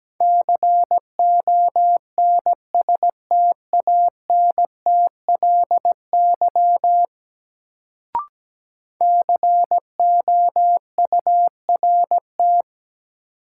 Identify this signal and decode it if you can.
Morse code, heard as a keyed tone, about 17 wpm, 700 Hz. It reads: CONSTANTLY COURT
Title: Sentences from Top 1600 Words - New Word Drill Alt - 17wpm